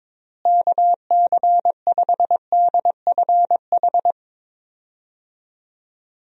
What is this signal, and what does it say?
Morse code, heard as a keyed tone, about 22 wpm, 700 Hz. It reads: KC5DFH
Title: USA Call Signs - 6 Characters - Repeated 3x - 22wpm